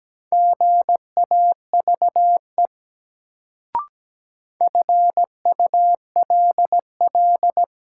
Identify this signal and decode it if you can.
Morse code, heard as a keyed tone, about 17 wpm, 700 Hz. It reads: GAVE FULL
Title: Sentences from Top 500 Words - New Word Drill - 17wpm